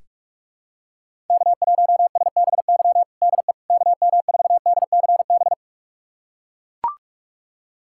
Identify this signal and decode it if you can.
Morse code, heard as a keyed tone, about 43 wpm, 700 Hz. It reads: K1SBY DE KM4DCB
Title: USA Callsign DE Callsign 45wpm